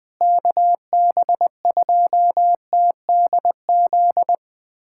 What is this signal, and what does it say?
Morse code, heard as a keyed tone, about 20 wpm, 700 Hz. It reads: KB2TDZ
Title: USA Callsign DE Callsign - 6 Characters - 20wpm